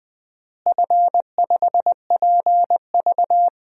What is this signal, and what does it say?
Morse code, heard as a keyed tone, about 20 wpm, 700 Hz. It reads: F5PV
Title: DXCC-100 Call Signs - 4 Characters - Repeated 2x - 20wpm